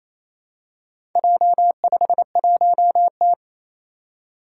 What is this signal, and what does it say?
Morse code, heard as a keyed tone, about 27 wpm, 700 Hz. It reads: J51T
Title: Four Challenging Letter-Number Combinations 28wpm